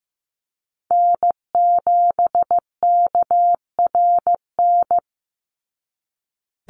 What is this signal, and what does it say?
Morse code, heard as a keyed tone, about 15 wpm, 700 Hz. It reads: N7KRN
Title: NA SOTA Chaser Callsigns 15wpm